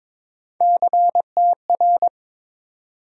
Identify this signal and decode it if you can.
Morse code, heard as a keyed tone, about 22 wpm, 700 Hz. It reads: CTR
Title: USA Call Sign Postfixes - 3 Letters 22wpm